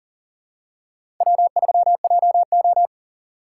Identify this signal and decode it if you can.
Morse code, heard as a keyed tone, about 40 wpm, 700 Hz. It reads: W3JO
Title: USA Call Signs - 4 Characters - 6x Character Spacing - 40wpm